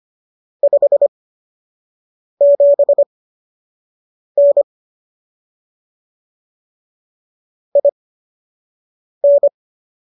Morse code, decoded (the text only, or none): 57N IN